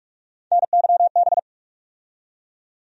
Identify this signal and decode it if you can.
Morse code, heard as a keyed tone, about 46 wpm, 700 Hz. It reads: NYB
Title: USA Call Sign Postfixes - 3 Letters 45wpm